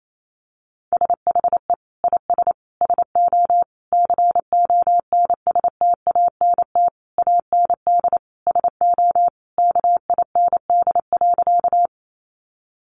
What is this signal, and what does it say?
Morse code, heard as a keyed tone, about 28 wpm, 700 Hz. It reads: SHE IS SO CONSTANT AND SO KIND.
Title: Sentences from Top 1500 Words 28wpm